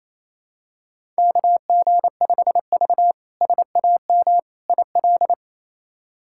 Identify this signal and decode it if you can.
Morse code, heard as a keyed tone, about 28 wpm, 700 Hz. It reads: KG5V SAM IL
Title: K1USN SST QSO Practice - 28wpm